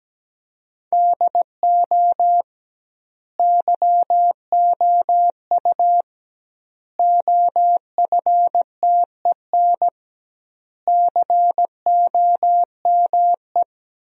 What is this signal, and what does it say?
Morse code, heard as a keyed tone, about 17 wpm, 700 Hz. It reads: DO YOU OFTEN COME
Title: 2x Word-Spacing - Sentences from Top 300 Words - 17wpm